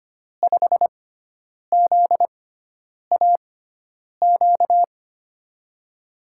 Morse code, heard as a keyed tone, about 25 wpm, spaced as for 9 wpm, 700 Hz. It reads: 5ZAQ
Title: USA Call Signs - 6x Character Spacing - 25wpm